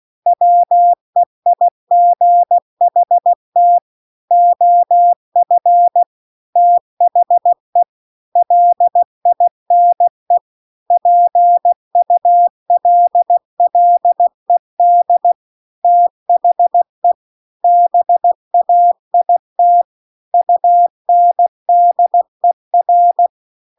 Morse code, heard as a keyed tone, about 16 wpm, 700 Hz. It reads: WEIGHT OF THE LINE PULLED THE BAIT UNDER
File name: CW-The-Big-Twoheaded-River-2-16WPM